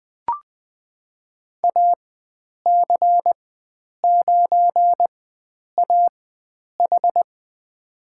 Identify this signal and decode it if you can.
Morse code, heard as a keyed tone, about 20 wpm, spaced as for 9 wpm, 700 Hz. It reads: AC9AH
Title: USA Call Signs - 4x Character Spacing - 20wpm